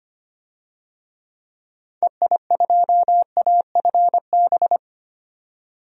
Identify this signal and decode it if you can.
Morse code, heard as a keyed tone, about 25 wpm, 700 Hz. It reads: EI2AFB
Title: DXCC-100 Call Signs - 6 Characters - 25wpm